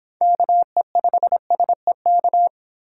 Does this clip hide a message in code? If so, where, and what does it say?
Yes: Morse code from 0.2 to 2.5 s, heard as a keyed tone, about 26 wpm, 700 Hz. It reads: KE5SEK